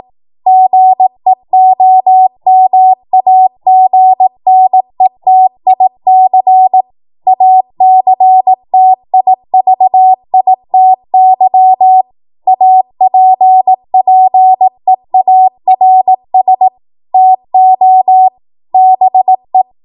Morse code, heard as a keyed tone, about 18 wpm, 750 Hz. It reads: GEOMAGNETIC ACTIVITY APPEARS TO BE
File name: Morse / 18WPM